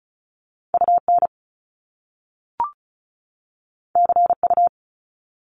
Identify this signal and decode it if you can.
Morse code, heard as a keyed tone, about 36 wpm, 700 Hz. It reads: UN CU
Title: Top Two Letter Combinations in Words 35wpm